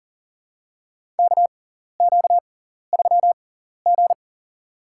Morse code, heard as a keyed tone, about 39 wpm, spaced as for 14 wpm, 700 Hz. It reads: KQ3G
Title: USA Call Signs - 4 Characters - 6x Character Spacing - 40wpm